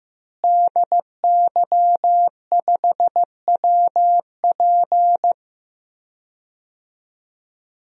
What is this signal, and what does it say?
Morse code, heard as a keyed tone, about 16 wpm, 700 Hz. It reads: DY5WP
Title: DXCC-100 Call Signs - 5 Characters - Repeated 3x - 15wpm